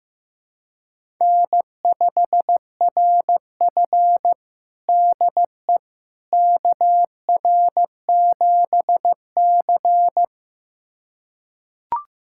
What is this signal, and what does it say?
Morse code, heard as a keyed tone, about 15 wpm, 700 Hz. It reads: N5RF DE KR7C
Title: USA Callsign DE Callsign - 4 Characters - 15wpm